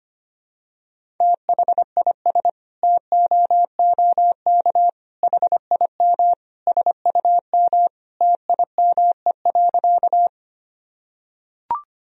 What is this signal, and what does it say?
Morse code, heard as a keyed tone, about 25 wpm, 700 Hz. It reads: THIS TOOK HIM SUM TIME.
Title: Sentences from Top 300 Words and Abbreviations 25wpm